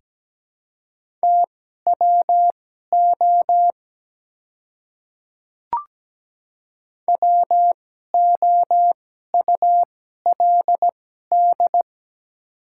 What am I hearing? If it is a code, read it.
Morse code, heard as a keyed tone, about 17 wpm, spaced as for 12 wpm, 690 Hz. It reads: TWO WOULD
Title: 2x Character Spacing - Top 100 Words - 17wpm